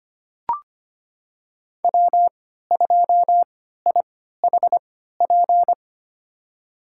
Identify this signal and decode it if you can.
Morse code, heard as a keyed tone, about 25 wpm, spaced as for 14 wpm, 700 Hz. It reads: W2IHP
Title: USA Call Signs - 5 Characters - 3x Character Spacing - 25wpm